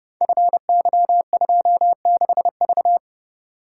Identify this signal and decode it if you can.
Morse code, heard as a keyed tone, about 30 wpm, 700 Hz. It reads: FY26V
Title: Code Groups - 5 Letter:Numbers - 30wpm